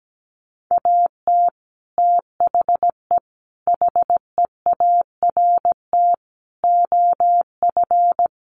Morse code, heard as a keyed tone, about 17 wpm, 700 Hz. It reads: AT THE HEART OF
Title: Sets of 4 Words 17wpm